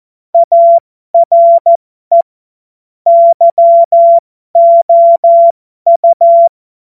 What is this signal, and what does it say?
Morse code, heard as a keyed tone, about 14 wpm, spaced as for 12 wpm, 680 Hz. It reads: ARE YOU
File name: Morsekurs / Witze